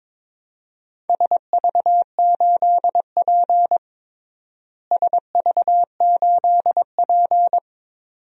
Morse code, heard as a keyed tone, about 22 wpm, 700 Hz. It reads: SV8P SV8P
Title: DXCC-100 Call Signs - 4 Characters - Repeated 2x - 22wpm